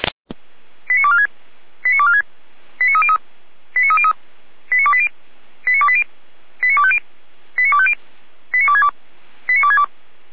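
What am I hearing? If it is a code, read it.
Five-tone selective call (ZVEI), heard as quick runs of five tones, tones of about 70 ms. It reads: 89246 89246 89303 89303 89280 89280 89250 89250 89262 89262
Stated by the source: ZVEI five-tone selective call 89229 received on SDR.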